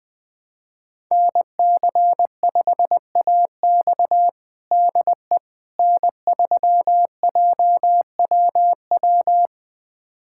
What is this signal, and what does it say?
Morse code, heard as a keyed tone, about 20 wpm, 700 Hz. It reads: NC5AX DE N3JWW
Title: USA Callsign DE Callsign - 5 Characters - 20wpm